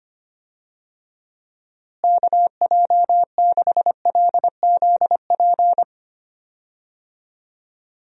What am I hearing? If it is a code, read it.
Morse code, heard as a keyed tone, about 25 wpm, 700 Hz. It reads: KJ6LZP
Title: USA Call Signs - 6 Characters - Repeated 2x - 25wpm